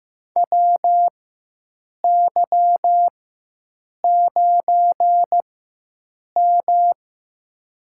Morse code, heard as a keyed tone, about 15 wpm, spaced as for 7 wpm, 700 Hz. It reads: WY9M
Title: USA Call Signs - 4 Characters - 4x Character Spacing - 15wpm